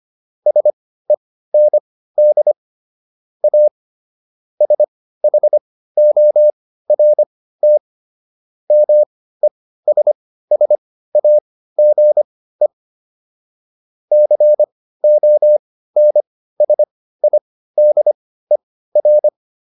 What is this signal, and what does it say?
Morse code, heard as a keyed tone, about 25 wpm, spaced as for 15 wpm, 600 Hz. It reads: SEND A SHORT MESSAGE CONSIDER